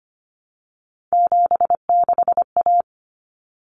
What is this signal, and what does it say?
Morse code, heard as a keyed tone, about 25 wpm, 700 Hz. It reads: 76A
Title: Common Cut Numbers - 3-Digit 25wpm